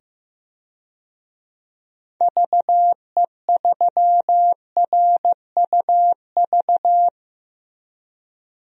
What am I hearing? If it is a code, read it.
Morse code, heard as a keyed tone, about 15 wpm, 700 Hz. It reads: VE3RUV
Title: DXCC-100 Call Signs - 6 Characters - 15wpm